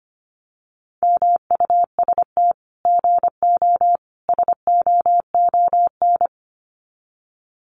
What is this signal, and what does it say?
Morse code, heard as a keyed tone, about 25 wpm, 700 Hz. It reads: MUST GO SOON
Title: Common QSO Phrases 25wpm